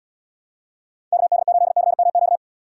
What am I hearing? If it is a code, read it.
Morse code, heard as a keyed tone, about 70 wpm, 700 Hz. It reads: KD8CMQ